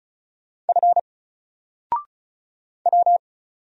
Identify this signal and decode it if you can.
Morse code, heard as a keyed tone, about 36 wpm, 700 Hz. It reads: F W
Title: Single Character (Letter:Number:Punctuation:Prosign) 35wpm